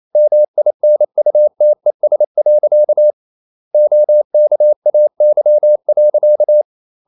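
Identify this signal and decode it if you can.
Morse code, heard as a keyed tone, about 28 wpm, 600 Hz. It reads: MINUTES. OKAY.
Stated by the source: Morse code practice at 28 wpm, text of Ken Follett's novel Never in English, from CW med Gnister.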